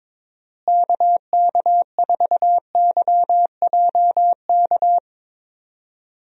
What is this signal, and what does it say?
Morse code, heard as a keyed tone, about 22 wpm, 700 Hz. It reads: KK4YJK
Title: USA Call Signs - 6 Characters - Repeated 2x - 22wpm